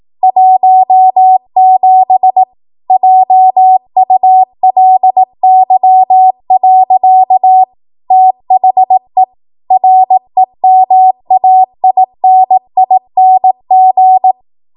Morse code, heard as a keyed tone, about 18 wpm, 750 Hz. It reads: 17 JULY. THE REMAINING
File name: Morse / 18WPM